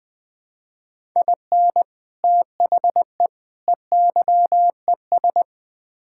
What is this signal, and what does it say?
Morse code, heard as a keyed tone, about 20 wpm, 700 Hz. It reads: IN THE EYES
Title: Build A Phrase - 5-Word Phrases - 20wpm